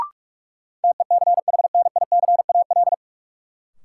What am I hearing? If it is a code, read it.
Morse code, heard as a keyed tone, about 42 wpm, 700 Hz. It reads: TECHNICAL
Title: Sentences from Top 1100 Words - New Word Drill Alt - 45wpm